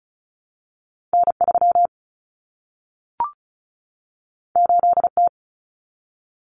Morse code, heard as a keyed tone, about 36 wpm, 700 Hz. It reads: N3 8T
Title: Common Cut Numbers - 2-Digit 35wpm